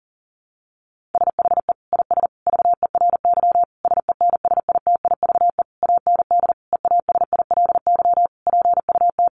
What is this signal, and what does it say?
Morse code, heard as a keyed tone, about 40 wpm, 700 Hz. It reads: SHE IS VERY SENSITIVE AND EASILY PUT
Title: Sentences from Top 1400 Words 40wpm